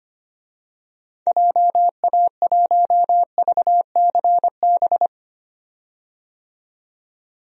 Morse code, heard as a keyed tone, about 25 wpm, 690 Hz. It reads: JA1VCB
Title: DXCC-100 Call Signs - 6 Characters - 25wpm